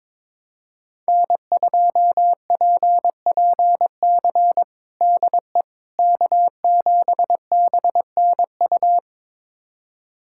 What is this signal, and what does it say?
Morse code, heard as a keyed tone, about 22 wpm, 700 Hz. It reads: N2PPC DE K7BNU
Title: USA Callsign DE Callsign - 5 Characters - 22wpm